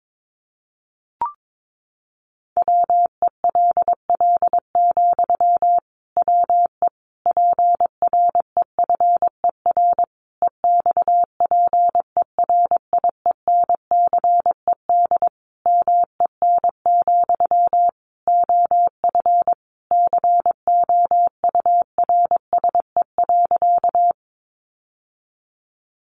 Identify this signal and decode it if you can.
Morse code, heard as a keyed tone, about 22 wpm, 700 Hz. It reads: WELL, WE PREFER EXPERIENCED MEN, OF COURSE.
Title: Sentences from Top 1100 Words 22wpm